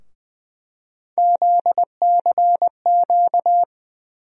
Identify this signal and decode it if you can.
Morse code, heard as a keyed tone, about 20 wpm, 700 Hz. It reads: ZCQ
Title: USA Call Sign Postfixes - 3 Letters 20wpm